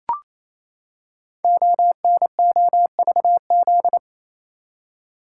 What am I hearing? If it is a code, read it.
Morse code, heard as a keyed tone, about 27 wpm, 700 Hz. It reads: ONOVZ
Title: Code Groups - 5 Letters - 28wpm